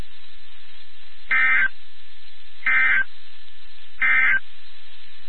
Emergency Alert System (SAME) tones, heard as harsh buzzing bursts. It is heard from 1.2 to 4.5 s.
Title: NOAA Weather Radio WXK79